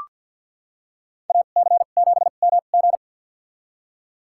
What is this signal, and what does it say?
Morse code, heard as a keyed tone, about 46 wpm, spaced as for 35 wpm, 690 Hz. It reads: AC7MG
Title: USA Call Signs - 2x Character Spacing - 50wpm